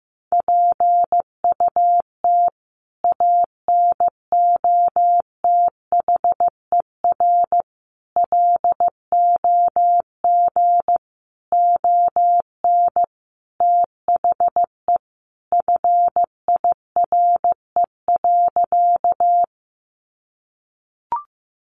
Morse code, heard as a keyed tone, about 15 wpm, 700 Hz. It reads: PUT ANOTHER LOG ON THE FIRE.
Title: Sentences from Top 1400 Words 15wpm